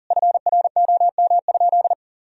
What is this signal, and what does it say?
Morse code, heard as a keyed tone, about 40 wpm, 700 Hz. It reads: FROM?
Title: Sentences from Top 1000 Words - Review Part 2 - 40wpm